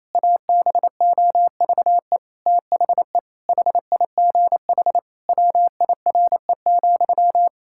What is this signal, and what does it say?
Morse code, heard as a keyed tone, about 28 wpm, 700 Hz. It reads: ABOVE THE HIGH WIRE,
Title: Pangram Challenge v2 - 28wpm